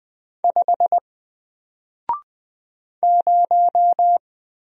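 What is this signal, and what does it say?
Morse code, heard as a keyed tone, about 20 wpm, 700 Hz. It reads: 5 0
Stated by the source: Single letter and number Morse drill at 20 wpm, from N4LCD.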